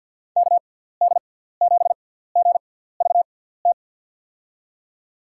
Morse code, heard as a keyed tone, about 53 wpm, spaced as for 17 wpm, 700 Hz. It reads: KD7GVT
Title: USA Call Signs - 6 Characters - 6x Character Spacing - 50wpm